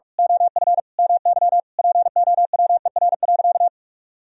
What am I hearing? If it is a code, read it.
Morse code, heard as a keyed tone, about 46 wpm, 700 Hz. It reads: OF MY POWER.
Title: Sentences from Top 1000 Words - Review Part 4 - 45wpm